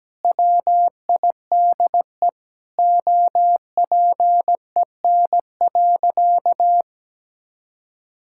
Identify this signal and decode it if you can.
Morse code, heard as a keyed tone, about 17 wpm, 700 Hz. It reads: WIDE OPEN.
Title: Sentences from Top 1000 Words - Review Part 6 - 17wpm